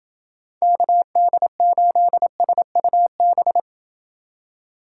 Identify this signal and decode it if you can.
Morse code, heard as a keyed tone, about 29 wpm, 700 Hz. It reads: KD8SUB